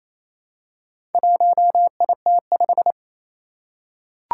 Morse code, heard as a keyed tone, about 27 wpm, 700 Hz. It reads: 1IT5
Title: Four Challenging Letter-Number Combinations 28wpm